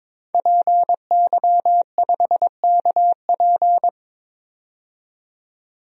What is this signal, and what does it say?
Morse code, heard as a keyed tone, about 22 wpm, 700 Hz. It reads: PY5KP
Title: DXCC-100 Call Signs - 5 Characters - 22wpm